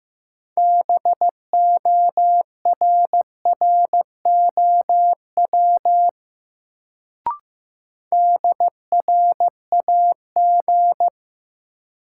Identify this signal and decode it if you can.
Morse code, heard as a keyed tone, about 15 wpm, 700 Hz. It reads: BORROW DRAG
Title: Sentences from Top 1500 Words - New Word Drill Alt - 15wpm